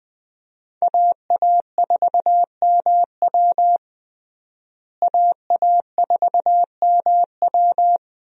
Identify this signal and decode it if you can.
Morse code, heard as a keyed tone, about 21 wpm, 700 Hz. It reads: AA4MW AA4MW
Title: USA Call Signs - 5 Characters - Repeated 2x - 20wpm